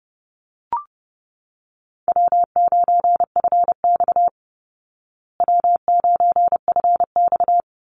Morse code, heard as a keyed tone, about 30 wpm, 700 Hz. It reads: W9FX W9FX
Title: Call Signs Repeated 3x - 30wpm